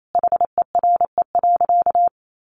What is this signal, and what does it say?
Morse code, heard as a keyed tone, about 28 wpm, 700 Hz. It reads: HERE.
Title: Sentences from Top 800 Words 28wpm